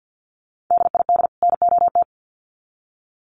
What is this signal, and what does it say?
Morse code, heard as a keyed tone, about 50 wpm, 700 Hz. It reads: DID NOT
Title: Sets of 2 Words 50wpm